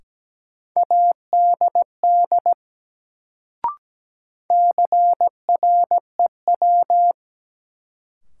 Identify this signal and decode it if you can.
Morse code, heard as a keyed tone, about 17 wpm, 700 Hz. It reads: ADD CREW
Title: Sentences from Top 1500 Words - New Word Drill Alt - 17wpm